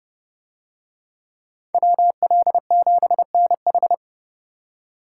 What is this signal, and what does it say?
Morse code, heard as a keyed tone, about 29 wpm, 700 Hz. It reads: WL7NH